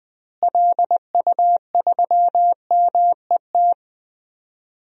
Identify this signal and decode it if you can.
Morse code, heard as a keyed tone, about 19 wpm, 700 Hz. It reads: LU3MET
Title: DXCC-100 Call Signs - 6 Characters - Repeated 3x - 20wpm